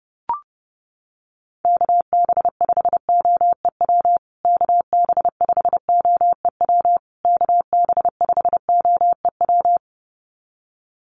Morse code, heard as a keyed tone, about 30 wpm, 700 Hz. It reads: KB5OEW KB5OEW KB5OEW
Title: USA Call Signs Repeated 3x 30wpm